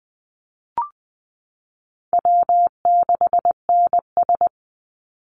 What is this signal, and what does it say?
Morse code, heard as a keyed tone, about 19 wpm, 700 Hz. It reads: W6NS
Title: Call Signs Repeated 3x - 20wpm